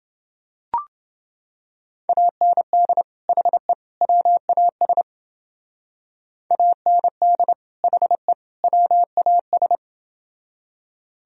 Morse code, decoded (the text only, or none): AND HE WAS AND HE WAS